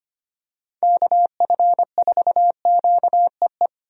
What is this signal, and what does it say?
Morse code, heard as a keyed tone, about 25 wpm, 700 Hz. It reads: KF4QEE
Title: USA Call Signs - 6 Characters - 25wpm